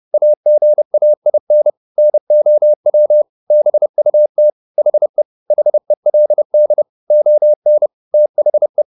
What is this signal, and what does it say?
Morse code, heard as a keyed tone, about 30 wpm, 600 Hz. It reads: AGAIN NOW BUT HE HELD ON THE